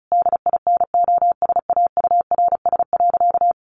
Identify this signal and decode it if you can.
Morse code, heard as a keyed tone, about 34 wpm, 700 Hz. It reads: DINOSAURS.
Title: A Sound Of Thunder 35wpm